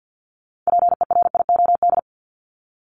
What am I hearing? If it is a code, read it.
Morse code, heard as a keyed tone, about 52 wpm, 700 Hz. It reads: PERIOD
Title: Top 501-750 Words 50wpm